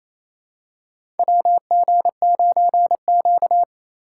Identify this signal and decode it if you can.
Morse code, heard as a keyed tone, about 28 wpm, 700 Hz. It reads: WG9Q
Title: USA Call Signs - 4x Character Spacing - 28wpm